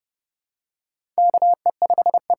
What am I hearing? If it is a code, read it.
Morse code, heard as a keyed tone, about 32 wpm, 700 Hz. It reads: KE5E
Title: USA Call Signs - 6 Characters - 6x Character Spacing - 30wpm